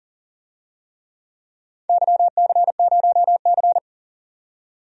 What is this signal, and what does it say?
Morse code, heard as a keyed tone, about 40 wpm, 700 Hz. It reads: YC0C
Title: DXCC-100 Call Signs - 4 Characters - 40wpm